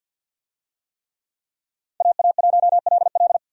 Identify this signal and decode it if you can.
Morse code, heard as a keyed tone, about 50 wpm, 700 Hz. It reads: AA1LL
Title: USA Call Signs - 5 Characters - 50wpm